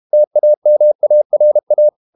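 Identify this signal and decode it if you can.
Morse code, heard as a keyed tone, about 32 wpm, 600 Hz. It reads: TAMARA
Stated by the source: Morse code practice at 32 wpm from CW med Gnister, reading Ken Follett's novel Never in English.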